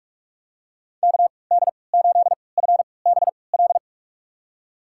Morse code, heard as a keyed tone, about 46 wpm, spaced as for 25 wpm, 700 Hz. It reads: KD8FBL
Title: USA Call Signs - 3x Character Spacing - 45wpm